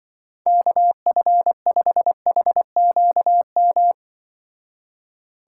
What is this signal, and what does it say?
Morse code, heard as a keyed tone, about 24 wpm, 700 Hz. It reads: KF5HQM
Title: Speed Racing - USA 2x3 Call Signs 20wpm